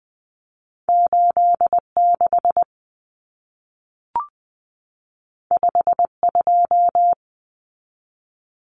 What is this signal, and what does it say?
Morse code, heard as a keyed tone, about 20 wpm, 700 Hz. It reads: 86 52
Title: Common Cut Numbers - 2-Digit 20wpm